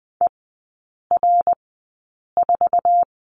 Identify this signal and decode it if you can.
Morse code, heard as a keyed tone, about 19 wpm, spaced as for 8 wpm, 700 Hz. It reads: ER4
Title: 074 Lesson 13e - Letter L - Three Characters 20wpm